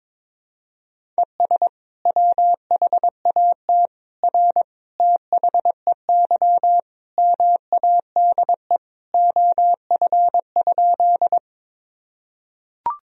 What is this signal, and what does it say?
Morse code, heard as a keyed tone, about 22 wpm, 700 Hz. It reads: ES WHAT R THEY MADE OF?
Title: Sentences from Top 200 Words and Abbreviations 22wpm